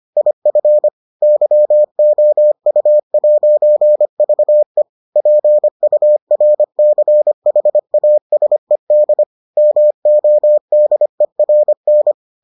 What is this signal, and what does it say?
Morse code, heard as a keyed tone, about 25 wpm, 600 Hz. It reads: IF YOU'VE PURCHASED MODERN